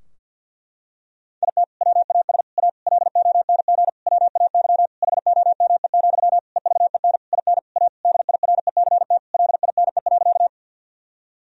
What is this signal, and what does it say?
Morse code, heard as a keyed tone, about 49 wpm, 700 Hz. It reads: IT WAS A LONG WAY HOME, EVEN IN A DIRECT LINE.
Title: Sentences from Top 1000 Words - Review Part 2 - 50wpm